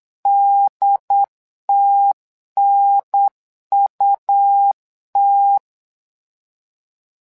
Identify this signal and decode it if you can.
Morse code, heard as a keyed tone, about 8 wpm, 800 Hz. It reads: DTNUT
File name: LEZIONI / 3 MWRO / 3 MWRO 8wpm 2606